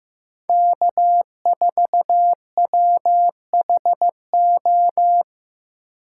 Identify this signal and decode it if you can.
Morse code, heard as a keyed tone, about 15 wpm, 700 Hz. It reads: K4WHO
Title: USA Call Signs - 5 Characters - 8x Character Spacing - 15wpm